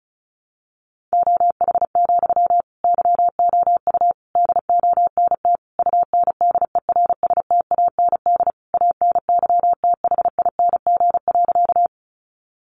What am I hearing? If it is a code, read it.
Morse code, heard as a keyed tone, about 34 wpm, 700 Hz. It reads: OH, YOU DONT UNDERSTAND ANYTHING.
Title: Sentences from Top 1000 Words 35wpm